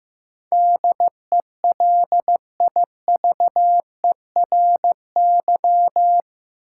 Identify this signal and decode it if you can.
Morse code, heard as a keyed tone, about 15 wpm, 700 Hz. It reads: DELIVERY
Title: Sentences from Top 1500 Words - New Word Drill - 15wpm